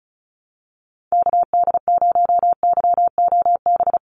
Morse code, heard as a keyed tone, about 36 wpm, 700 Hz. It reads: KD0YOB
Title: NA SOTA Activator Callsigns 35wpm